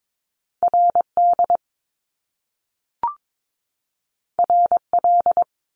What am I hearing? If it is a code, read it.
Morse code, heard as a keyed tone, about 22 wpm, 700 Hz. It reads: RD RL
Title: Top Two Letter Combinations in Words 22wpm